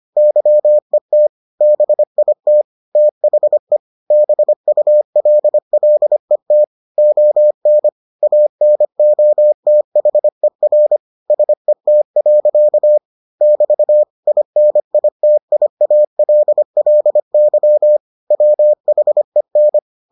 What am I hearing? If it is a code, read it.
Morse code, heard as a keyed tone, about 25 wpm, 600 Hz. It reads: YET BIT THE BULLET ON ANOTHER SET. <BT> INITIALLY WHEN